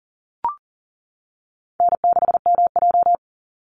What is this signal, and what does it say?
Morse code, heard as a keyed tone, about 40 wpm, 700 Hz. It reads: N6MJ
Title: Easy Call Signs - 40wpm